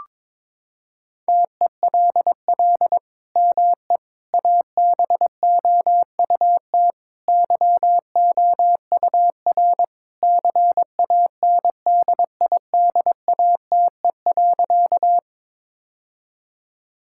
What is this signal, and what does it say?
Morse code, heard as a keyed tone, about 22 wpm, 700 Hz. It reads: TELL ME ABOUT YOUR CANDIDATE.
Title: Sentences from Top 1600 Words 22wpm